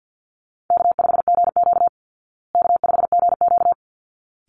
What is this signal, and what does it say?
Morse code, heard as a keyed tone, about 50 wpm, 700 Hz. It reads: K5GQ K5GQ
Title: Easy Call Signs Repeated 2x - 50wpm